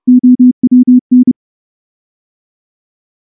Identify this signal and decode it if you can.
Morse code, heard as a keyed tone, about 28 wpm, 250 Hz. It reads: OWN
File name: morse / morse-training-30WPM